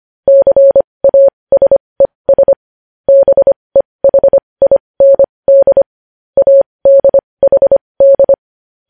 Morse code, heard as a keyed tone, about 25 wpm, spaced as for 20 wpm, 550 Hz. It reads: CASES BEHIND ADHD
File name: podcast / News headlines-1x-25x20